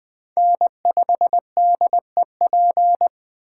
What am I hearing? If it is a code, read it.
Morse code, heard as a keyed tone, about 21 wpm, 700 Hz. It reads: N5DEP
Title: Speed Racing - USA 1x3 Call Signs 17wpm